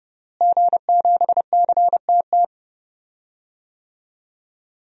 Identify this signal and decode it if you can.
Morse code, heard as a keyed tone, about 30 wpm, 700 Hz. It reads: G7CTT
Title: DXCC-100 Call Signs - 5 Characters - Repeated 3x - 30wpm